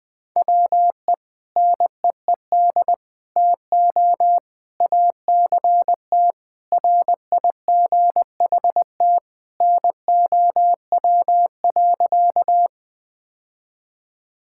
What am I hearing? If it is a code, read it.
Morse code, heard as a keyed tone, about 21 wpm, 700 Hz. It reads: WE NEED TO ACT RIGHT NOW.
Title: Sentences with 2-Word Phrases - Spoken Beginning - 20wpm